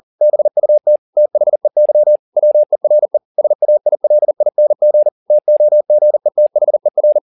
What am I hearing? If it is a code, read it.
Morse code, heard as a keyed tone, about 40 wpm, 600 Hz. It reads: BUT THEY WERE SAILING TOGETHER
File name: the old man and the sea 2 40wpm